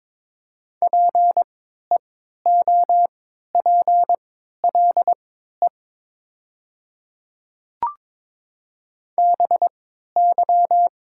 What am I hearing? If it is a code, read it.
Morse code, heard as a keyed tone, about 22 wpm, spaced as for 12 wpm, 700 Hz. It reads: PEOPLE BY
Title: 3x Character Spacing - Top 100 Words - 22wpm